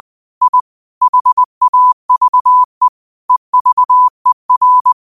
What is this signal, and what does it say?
Morse code, heard as a keyed tone, about 20 wpm, 1000 Hz. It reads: I HAVE EVER
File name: qotdfiles / 20wpm